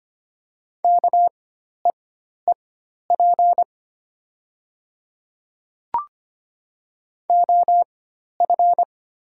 Morse code, heard as a keyed tone, about 25 wpm, spaced as for 12 wpm, 700 Hz. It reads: KEEP OF
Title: 4x Character Spacing - Top 300 Words - 25wpm